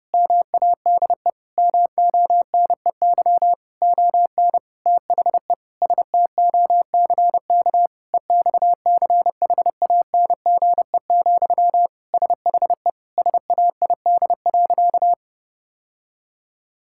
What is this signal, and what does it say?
Morse code, heard as a keyed tone, about 30 wpm, 700 Hz. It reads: MADE MONEY ON THE STOCK EXCHANGE, SHE SAID.
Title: Sentences from Top 1000 Words - Review Part 6 - 30wpm